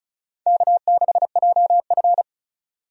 Morse code, heard as a keyed tone, about 34 wpm, 700 Hz. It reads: K6JF
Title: USA Call Signs - 4 Characters - 35wpm